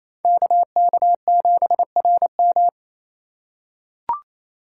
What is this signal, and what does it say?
Morse code, heard as a keyed tone, about 28 wpm, 700 Hz. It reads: KK7RM
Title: USA Call Signs - 5 Characters - 4x Character Spacing - 28wpm